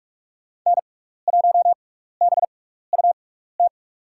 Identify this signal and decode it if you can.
Morse code, heard as a keyed tone, about 47 wpm, spaced as for 16 wpm, 700 Hz. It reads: N1BUT
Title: USA Call Signs - 5 Characters - 6x Character Spacing - 45wpm